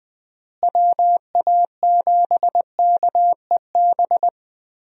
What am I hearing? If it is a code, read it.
Morse code, heard as a keyed tone, about 20 wpm, 700 Hz. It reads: WA7KEB
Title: USA Call Signs - 6 Characters - Repeated 2x - 20wpm